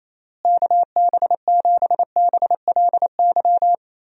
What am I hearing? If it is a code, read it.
Morse code, heard as a keyed tone, about 28 wpm, 700 Hz. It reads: KB7BLY
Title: USA Call Signs - 6 Characters - Repeated 3x - 28wpm